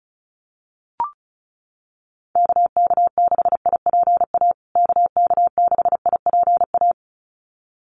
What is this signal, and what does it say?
Morse code, heard as a keyed tone, about 34 wpm, 700 Hz. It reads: KK6IPA KK6IPA
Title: USA Call Signs Repeated 2x 35wpm